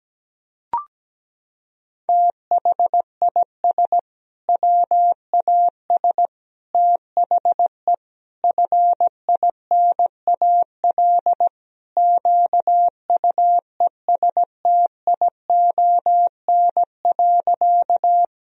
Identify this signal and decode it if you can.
Morse code, heard as a keyed tone, about 17 wpm, 700 Hz. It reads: THIS WAS THE FINAL QUESTION.
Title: Sentences from Top 1000 Words - Review Part 2 - 17wpm